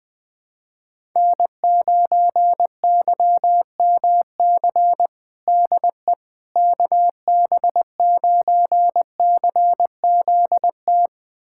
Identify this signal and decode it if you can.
Morse code, heard as a keyed tone, about 21 wpm, 700 Hz. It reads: N9YMC DE KB9CZT
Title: USA Callsign DE Callsign 20wpm